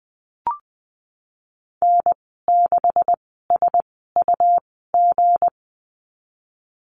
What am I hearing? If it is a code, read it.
Morse code, heard as a keyed tone, about 20 wpm, spaced as for 15 wpm, 700 Hz. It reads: N6SUG
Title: 186 Lesson 30g - Letter b - Call Signs 20wpm